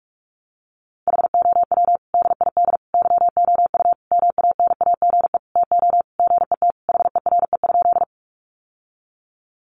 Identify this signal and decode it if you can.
Morse code, heard as a keyed tone, about 44 wpm, 700 Hz. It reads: HOW DID YOU MANAGE TO GET HERE?
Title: Sentences from Top 1400 Words 45wpm